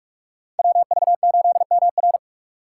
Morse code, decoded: WV8MR

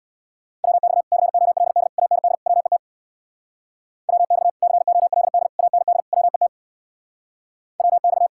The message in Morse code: WX COLD AND WET WX COLD AND WET WX

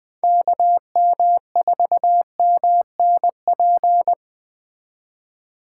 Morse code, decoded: KM4MNP

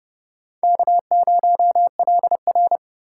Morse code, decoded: K0LR

